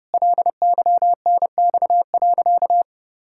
Morse code, heard as a keyed tone, about 30 wpm, 700 Hz. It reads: LYNX.
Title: Pangram Challenge v2 - 30wpm